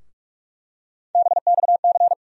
Morse code, heard as a keyed tone, about 46 wpm, 700 Hz. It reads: BXC
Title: USA Call Sign Postfixes - 3 Letters 45wpm